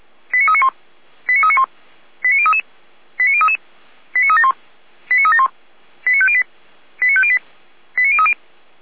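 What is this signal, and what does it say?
Five-tone selective call (ZVEI), heard as quick runs of five tones, tones of about 80 ms. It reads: 89391 89391 89033 89033 89371 89371 89508 89508 89030
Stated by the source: ZVEI five-tone selective call 89125, received on an SDR.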